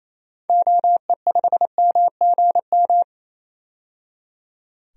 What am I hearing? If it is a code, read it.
Morse code, heard as a keyed tone, about 27 wpm, 700 Hz. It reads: OE5MGM
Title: DXCC-100 Call Signs - 6 Characters - 28wpm